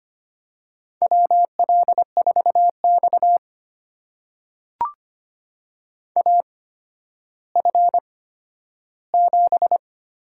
Morse code, heard as a keyed tone, about 25 wpm, spaced as for 12 wpm, 700 Hz. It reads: WL4X AF7
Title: USA Call Signs - 8x Character Spacing - 25wpm